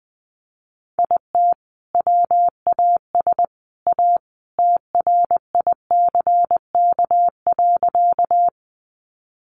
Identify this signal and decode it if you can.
Morse code, heard as a keyed tone, about 20 wpm, 700 Hz. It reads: IT WAS A TRICK.
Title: Sentences from Top 1500 Words 20wpm